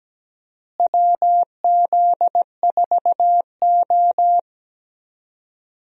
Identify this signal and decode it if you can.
Morse code, heard as a keyed tone, about 17 wpm, 700 Hz. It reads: WZ4O
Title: USA Call Signs - 4 Characters - Repeated 2x - 17wpm